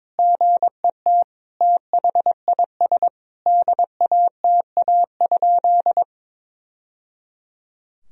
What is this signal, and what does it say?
Morse code, heard as a keyed tone, about 22 wpm, 700 Hz. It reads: GET THIS DATA?
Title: Sentences from Top 1000 Words - Review Part 3 - 22wpm